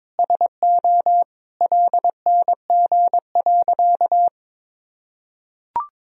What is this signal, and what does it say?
Morse code, heard as a keyed tone, about 22 wpm, 700 Hz. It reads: SO LNG.
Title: Sentences from Top 100 Words and Abbreviations 22wpm